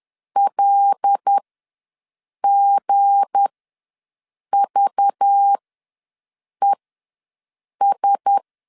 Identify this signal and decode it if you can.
Morse code, heard as a keyed tone, about 10 wpm, spaced as for 6 wpm, 800 Hz. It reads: LGVES
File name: LEZIONI / 7 GF9Z / 7 GF ws 10wpm 3918